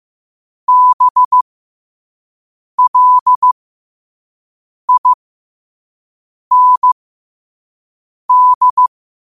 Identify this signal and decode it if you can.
Morse code, heard as a keyed tone, about 15 wpm, spaced as for 5 wpm, 1000 Hz. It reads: BLIND
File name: qotdfiles / 5wpm